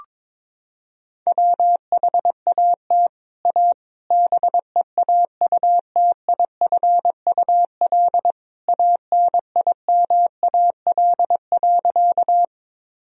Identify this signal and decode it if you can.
Morse code, heard as a keyed tone, about 22 wpm, 700 Hz. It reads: WHAT A BEAUTIFUL ANIMAL.
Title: Sentences from Top 1000 Words - Review Part 1 - 22wpm